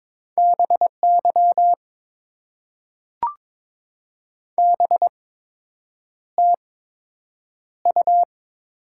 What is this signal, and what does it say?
Morse code, heard as a keyed tone, about 22 wpm, spaced as for 8 wpm, 700 Hz. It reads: BY BTU
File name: Top 90 QSO Elements - 8x Character Spacing - 22wpm